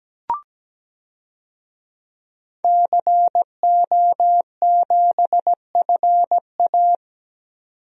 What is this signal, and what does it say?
Morse code, heard as a keyed tone, about 17 wpm, 700 Hz. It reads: CO7FA
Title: DXCC-100 Call Signs - 5 Characters - 17wpm